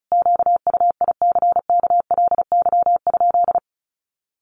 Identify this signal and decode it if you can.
Morse code, heard as a keyed tone, about 34 wpm, 700 Hz. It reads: QUICKLY?
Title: Sentences from Top 700 Words 35wpm